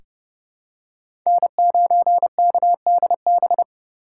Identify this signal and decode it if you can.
Morse code, heard as a keyed tone, about 30 wpm, 700 Hz. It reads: N9KDB
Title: K1USN SST QSO Practice - 30wpm